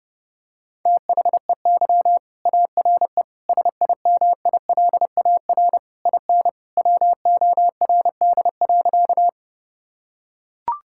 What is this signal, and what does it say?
Morse code, heard as a keyed tone, about 30 wpm, 700 Hz. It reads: THEY ARE SIMILAR IN WORD.
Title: Sentences from Top 1000 Words - Review Part 5 - 30wpm